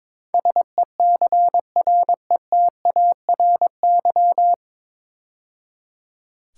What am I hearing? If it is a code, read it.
Morse code, heard as a keyed tone, about 22 wpm, 700 Hz. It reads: SECRETARY
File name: 9 Letter Words - First 3 Letters Given - 22wpm